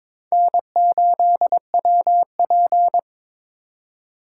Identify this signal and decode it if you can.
Morse code, heard as a keyed tone, about 26 wpm, 700 Hz. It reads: N8WP
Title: Speed Racing - USA 1x2 Call Signs 22wpm